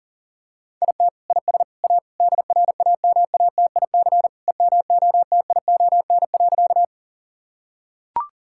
Morse code, heard as a keyed tone, about 40 wpm, 700 Hz. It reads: IT IS A DRAMATIC EMOTION.
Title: Sentences from Top 1600 Words 40wpm